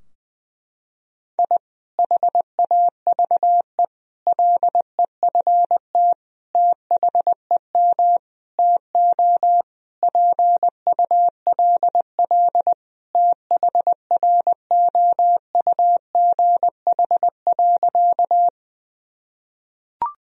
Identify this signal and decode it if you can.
Morse code, heard as a keyed tone, about 20 wpm, 700 Hz. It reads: I HAVE LEFT THEM TO PULL THROUGH.
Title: Sentences from Top 1000 Words - Review Part 2 - 20wpm